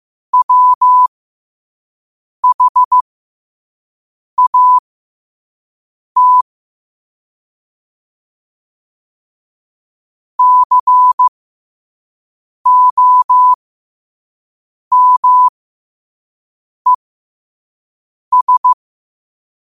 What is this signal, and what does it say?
Morse code, heard as a keyed tone, about 15 wpm, spaced as for 5 wpm, 1000 Hz. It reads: WHAT COMES
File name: qotdfiles / 5wpm